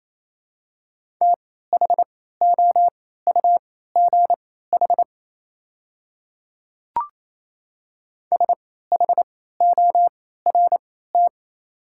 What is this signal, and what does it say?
Morse code, heard as a keyed tone, about 27 wpm, spaced as for 16 wpm, 700 Hz. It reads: THOUGH SHORT
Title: 3x Character Spacing - Top 500 Words - 28wpm